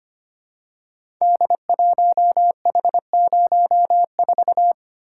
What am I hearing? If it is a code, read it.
Morse code, heard as a keyed tone, about 25 wpm, 700 Hz. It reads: D1H04
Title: Code Groups - 5 Letter:Numbers - 25wpm